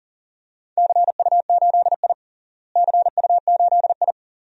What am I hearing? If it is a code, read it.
Morse code, heard as a keyed tone, about 40 wpm, 700 Hz. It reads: CU8I CU8I
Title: DXCC-100 Call Signs - 4 Characters - Repeated 3x - 40wpm